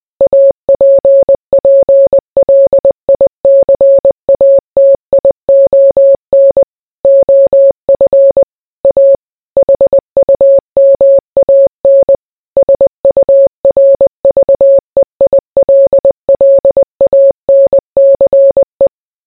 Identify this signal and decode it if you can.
Morse code, heard as a keyed tone, about 20 wpm, 550 Hz. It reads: APPLICATION OF A HUMAN SURVEILLANCE